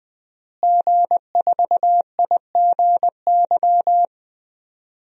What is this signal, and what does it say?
Morse code, heard as a keyed tone, about 20 wpm, 700 Hz. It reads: G4IGY
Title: DXCC-100 Call Signs - 5 Characters - Repeated 2x - 20wpm